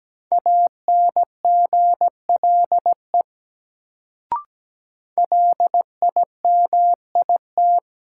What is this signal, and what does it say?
Morse code, heard as a keyed tone, about 17 wpm, 700 Hz. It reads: ANGLE LIMIT
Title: Sentences from Top 1500 Words - New Word Drill Alt - 17wpm